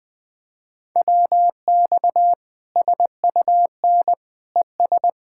Morse code, heard as a keyed tone, about 20 wpm, 700 Hz. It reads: WX SUN ES
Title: Practice QSO - Exchange 2 - Info - 20wpm